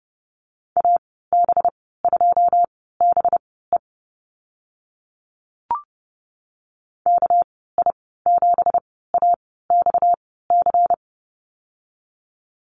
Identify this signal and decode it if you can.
Morse code, heard as a keyed tone, about 30 wpm, spaced as for 17 wpm, 700 Hz. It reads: AB2BE KI7AXC
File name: Lesson 37g - Letter x - Call Signs 30wpm